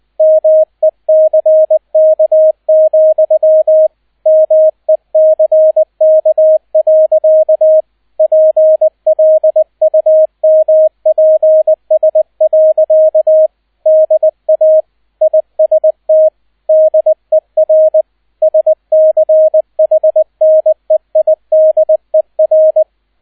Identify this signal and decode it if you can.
Morse code, heard as a keyed tone, about 20 wpm, 630 Hz, with radio noise. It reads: MECK, MECK. PLUMPS. DA IST DER SCHNEIDER